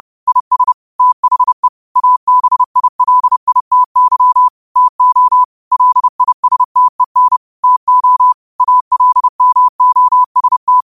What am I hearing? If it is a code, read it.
Morse code, heard as a keyed tone, about 30 wpm, 1000 Hz. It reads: IS THE ABILITY TO LISTEN TO ALMOST
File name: qotdfiles / 30wpm